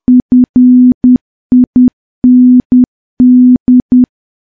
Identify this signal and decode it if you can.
Morse code, heard as a keyed tone, about 10 wpm, 250 Hz. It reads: FIND
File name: morse / morse-training-10WPM